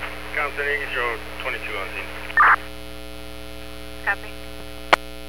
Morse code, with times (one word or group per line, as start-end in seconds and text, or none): none